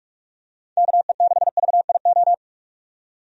Morse code, heard as a keyed tone, about 44 wpm, 700 Hz. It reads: KE6VIO
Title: USA Call Signs - 6 Characters - 45wpm